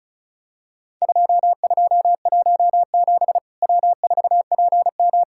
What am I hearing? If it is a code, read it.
Morse code, heard as a keyed tone, about 36 wpm, 700 Hz. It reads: 2217 W4PM
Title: Worked All Europe DX Contest - QTC Exchange - 35wpm